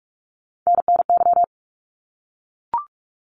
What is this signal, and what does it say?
Morse code, heard as a keyed tone, about 44 wpm, 700 Hz. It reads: NNY
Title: ARRL Sections 45wpm